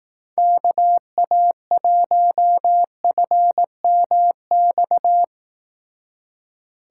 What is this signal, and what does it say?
Morse code, heard as a keyed tone, about 18 wpm, 700 Hz. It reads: KA1FMX
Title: Speed Racing - USA 2x3 Call Signs 15wpm